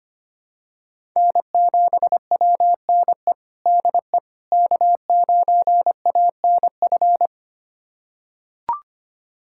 Morse code, heard as a keyed tone, about 25 wpm, 700 Hz. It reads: N7WNE DE K9ANF
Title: USA Callsign DE Callsign - 5 Characters - 25wpm